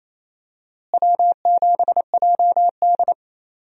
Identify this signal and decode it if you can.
Morse code, heard as a keyed tone, about 28 wpm, 700 Hz. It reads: W7JD
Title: USA Call Signs - 4 Characters - Repeated 3x - 28wpm